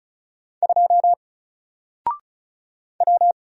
Morse code, heard as a keyed tone, about 36 wpm, 700 Hz. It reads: 2 W
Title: Single Letter-Number Alt 35wpm